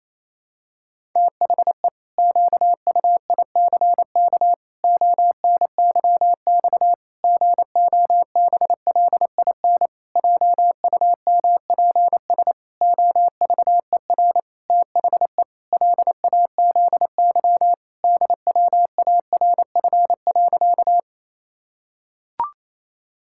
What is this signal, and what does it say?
Morse code, heard as a keyed tone, about 27 wpm, 700 Hz. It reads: THE QUICK ONYX GOBLIN JUMPS OVER THE LAZY DWARF.
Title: Pangram Challenge v2 - 28wpm